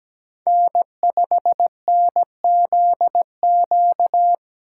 Morse code, heard as a keyed tone, about 17 wpm, 700 Hz. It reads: N5NZQ
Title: USA Call Signs - 5 Characters - 4x Character Spacing - 17wpm